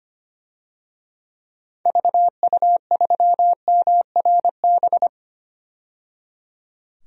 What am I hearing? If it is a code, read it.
Morse code, heard as a keyed tone, about 25 wpm, 700 Hz. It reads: VU3MRB
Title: DXCC-100 Call Signs - 6 Characters - 25wpm